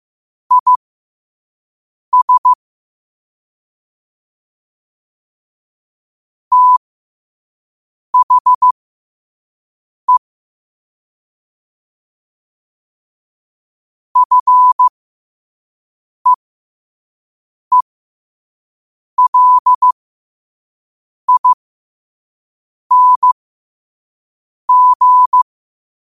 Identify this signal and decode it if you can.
Morse code, heard as a keyed tone, about 15 wpm, spaced as for 5 wpm, 1000 Hz. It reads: IS THE FEELING
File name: qotdfiles / 5wpm